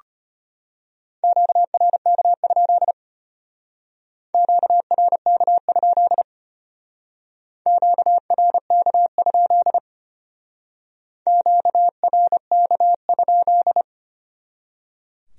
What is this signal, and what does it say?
Morse code, heard as a keyed tone, about 33 wpm, 700 Hz. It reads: QRK? QRK? QRK? QRK?